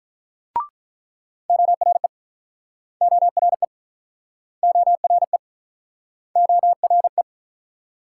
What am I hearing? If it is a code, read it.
Morse code, heard as a keyed tone, about 48 wpm, 700 Hz. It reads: ORE ORE ORE ORE